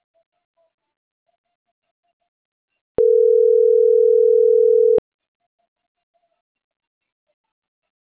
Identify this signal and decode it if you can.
Ringback tone (North America), 440 + 480 Hz.